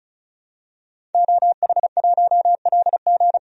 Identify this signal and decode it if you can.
Morse code, heard as a keyed tone, about 36 wpm, 700 Hz. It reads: OH1LG
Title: DXCC-100 Call Signs - 5 Characters - 35wpm